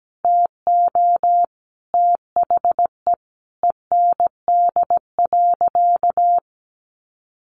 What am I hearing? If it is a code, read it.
Morse code, heard as a keyed tone, about 17 wpm, 700 Hz. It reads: TO THE END.
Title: Sentences from Top 1300 Words 17wpm